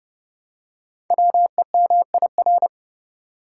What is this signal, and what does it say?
Morse code, heard as a keyed tone, about 30 wpm, 700 Hz. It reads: WEMIR
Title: Code Groups - 5 Letters - 30wpm